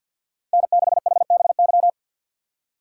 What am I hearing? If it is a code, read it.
Morse code, heard as a keyed tone, about 51 wpm, 700 Hz. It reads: N6HBY
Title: USA Call Signs - 5 Characters - 2x Character Spacing - 50wpm